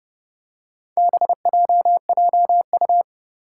Morse code, heard as a keyed tone, about 31 wpm, 700 Hz. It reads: BJJU
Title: Four Challenging Letter-Number Combinations 30wpm